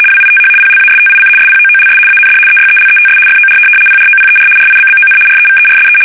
RTTY, about 75 baud. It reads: WITH PLASTIC SPOOLS AND EYELETS FOR PROPER RIBBON REVERSE
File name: fsk850-75B / teletype-corp